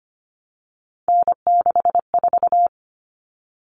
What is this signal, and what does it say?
Morse code, heard as a keyed tone, about 25 wpm, 700 Hz. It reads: N64
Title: Common Cut Numbers - 3-Digit 25wpm